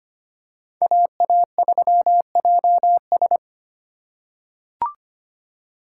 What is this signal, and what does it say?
Morse code, heard as a keyed tone, about 25 wpm, 700 Hz. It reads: AA3JS
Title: USA Call Signs - 5 Characters - 4x Character Spacing - 25wpm